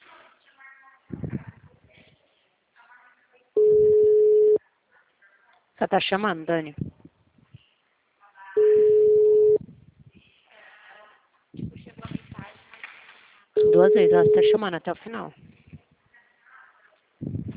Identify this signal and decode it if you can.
Ringback tone (Europe, 425 Hz), with speech.
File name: out- -2048- - - .25079